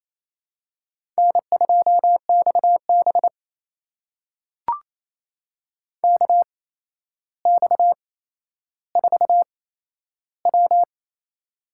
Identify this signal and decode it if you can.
Morse code, heard as a keyed tone, about 27 wpm, spaced as for 12 wpm, 700 Hz. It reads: N2XB KX4W
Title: USA Call Signs - 4 Characters - 8x Character Spacing - 28wpm